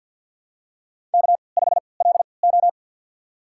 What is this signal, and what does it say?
Morse code, heard as a keyed tone, about 52 wpm, spaced as for 28 wpm, 700 Hz. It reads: K5LO
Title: USA Call Signs - 4 Characters - 3x Character Spacing - 50wpm